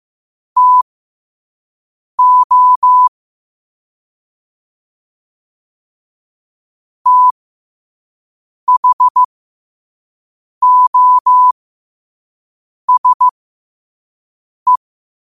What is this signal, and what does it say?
Morse code, heard as a keyed tone, about 15 wpm, spaced as for 5 wpm, 1000 Hz. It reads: TO THOSE
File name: qotdfiles / 5wpm